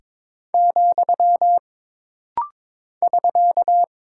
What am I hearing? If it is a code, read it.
Morse code, heard as a keyed tone, about 22 wpm, 700 Hz. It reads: , <SK>
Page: This practice set is composed of Letters, Numbers, common punctuations, and common prosigns.